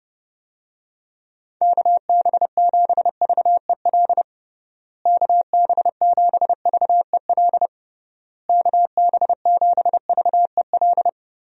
Morse code, heard as a keyed tone, about 29 wpm, 700 Hz. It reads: KB7VEL KB7VEL KB7VEL
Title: USA Call Signs - 6 Characters - Repeated 3x - 30wpm